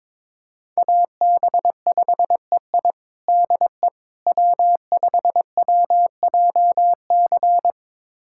Morse code, heard as a keyed tone, about 22 wpm, 700 Hz. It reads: AB5EI DE W5WJC
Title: USA Callsign DE Callsign - 5 Characters - 22wpm